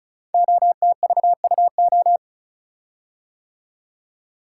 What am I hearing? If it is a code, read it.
Morse code, heard as a keyed tone, about 36 wpm, 700 Hz. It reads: OTVUO
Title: Code Groups - 5 Letter:Numbers - 35wpm